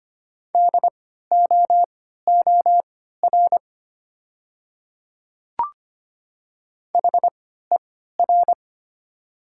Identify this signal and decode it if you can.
Morse code, heard as a keyed tone, about 25 wpm, spaced as for 14 wpm, 700 Hz. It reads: DOOR HER
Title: 3x Character Spacing - Top 300 Words - 25wpm